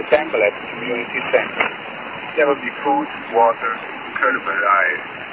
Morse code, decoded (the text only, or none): none